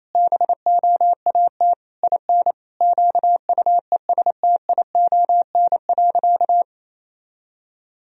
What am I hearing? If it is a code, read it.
Morse code, heard as a keyed tone, about 28 wpm, 700 Hz. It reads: BOAT IN QUESTION.
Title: Sentences from Top 1000 Words - Review Part 3 - 28wpm